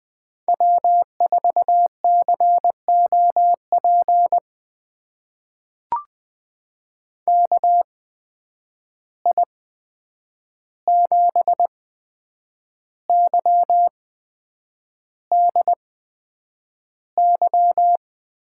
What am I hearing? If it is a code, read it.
Morse code, heard as a keyed tone, about 21 wpm, spaced as for 8 wpm, 700 Hz. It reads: W4COP KI7YDY
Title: USA Call Signs - 8x Character Spacing - 20wpm